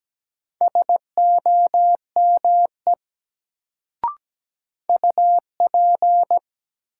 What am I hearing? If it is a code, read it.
Morse code, heard as a keyed tone, about 17 wpm, 700 Hz. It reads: SOME UP
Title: Sentences from Top 100 Words - New Word Drill - 17wpm